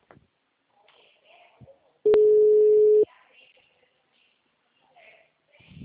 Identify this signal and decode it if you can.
Ringback tone (Europe, 425 Hz).
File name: out- -2048- - - .5004